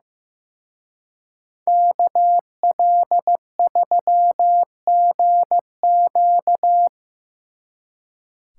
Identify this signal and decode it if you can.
Morse code, heard as a keyed tone, about 17 wpm, 700 Hz. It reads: KL3GQ
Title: Speed Racing - USA 2x2 Call Signs 15wpm